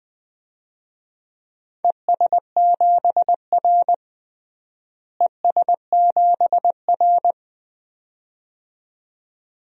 Morse code, heard as a keyed tone, about 20 wpm, 700 Hz. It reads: ES7R ES7R
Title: DXCC-100 Call Signs - 4 Characters - Repeated 2x - 20wpm